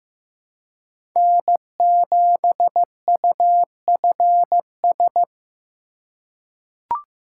Morse code, heard as a keyed tone, about 15 wpm, 700 Hz. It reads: N7UFS
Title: USA Call Signs - 5 Characters - 4x Character Spacing - 15wpm